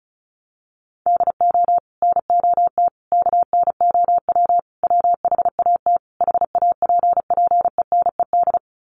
Morse code, heard as a keyed tone, about 34 wpm, 700 Hz. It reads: DO NOT KNOW WHAT HAPPENED
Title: Sets of 5 Words Encore2 35wpm